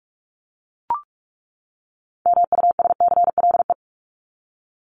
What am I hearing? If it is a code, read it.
Morse code, heard as a keyed tone, about 44 wpm, 700 Hz. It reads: MUSCLE
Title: Top 1251-1500 Words 45wpm